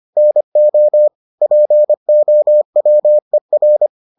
Morse code, heard as a keyed tone, about 25 wpm, 600 Hz. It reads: NO POWER